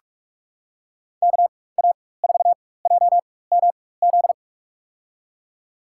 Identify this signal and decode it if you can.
Morse code, heard as a keyed tone, about 46 wpm, spaced as for 21 wpm, 700 Hz. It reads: KA4JMZ
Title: USA Call Signs - 6 Characters - 4x Character Spacing - 45wpm